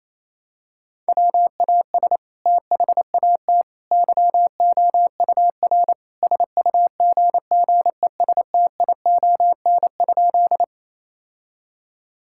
Morse code, heard as a keyed tone, about 28 wpm, 700 Hz. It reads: WAS THAT YOUR SUGGESTION?
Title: Sentences from Top 1600 Words 28wpm